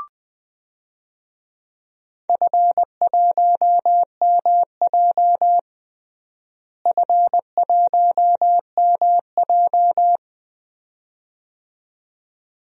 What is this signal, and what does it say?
Morse code, heard as a keyed tone, about 20 wpm, 700 Hz. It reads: F1MJ F1MJ
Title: DXCC-100 Call Signs - 4 Characters - Repeated 2x - 20wpm